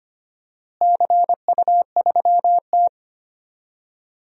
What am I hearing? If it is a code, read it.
Morse code, heard as a keyed tone, about 25 wpm, 700 Hz. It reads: CU3T
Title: DXCC-100 Call Signs - 4 Characters - Repeated 2x - 25wpm